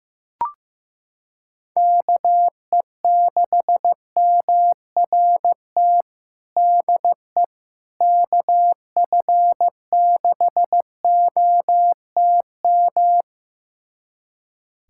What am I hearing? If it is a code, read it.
Morse code, heard as a keyed tone, about 15 wpm, 700 Hz. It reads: KE6MRT DE KF6OTM
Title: USA Callsign DE Callsign - 6 Characters - 15wpm